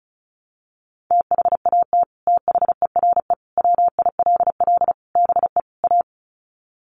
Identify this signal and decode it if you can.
Morse code, heard as a keyed tone, about 34 wpm, 700 Hz. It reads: THAT THERE WILL BE A
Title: Sets of 5 Words Encore2 35wpm